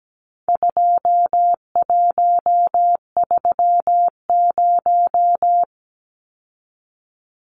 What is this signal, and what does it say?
Morse code, heard as a keyed tone, about 17 wpm, 700 Hz. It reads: 2130
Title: Numbers - 4-Digit 17wpm